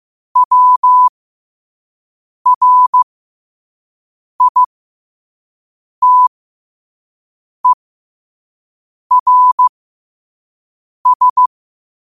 Morse code, heard as a keyed tone, about 15 wpm, spaced as for 5 wpm, 1000 Hz. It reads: WRITERS
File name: qotdfiles / 5wpm